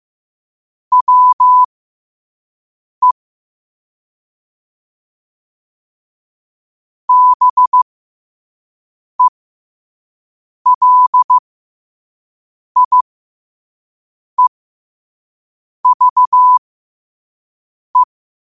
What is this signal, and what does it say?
Morse code, heard as a keyed tone, about 15 wpm, spaced as for 5 wpm, 1000 Hz. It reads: WE BELIEVE
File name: qotdfiles / 5wpm